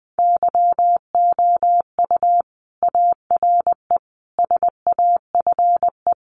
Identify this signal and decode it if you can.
Morse code, heard as a keyed tone, about 20 wpm, 700 Hz. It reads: YOU ARE SAFE
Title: Sentences from Top 900 Words 20wpm